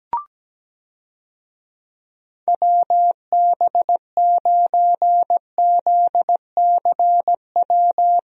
Morse code, heard as a keyed tone, about 17 wpm, 700 Hz. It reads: WB9ZCW
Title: USA Call Signs - 6 Characters - Repeated 2x - 17wpm